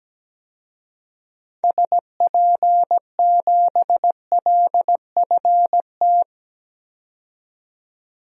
Morse code, heard as a keyed tone, about 17 wpm, 700 Hz. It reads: SP7LFT
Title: DXCC-100 Call Signs - 6 Characters - 17wpm